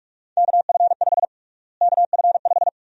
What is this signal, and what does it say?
Morse code, heard as a keyed tone, about 46 wpm, 700 Hz. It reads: KF5 KF5
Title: USA Call Sign Prefixes - Repeated 2x - 45wpm